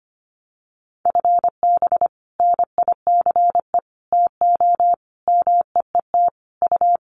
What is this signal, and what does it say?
Morse code, heard as a keyed tone, about 25 wpm, 700 Hz. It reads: FB NICE TO MEET U
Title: Common QSO Phrases 25wpm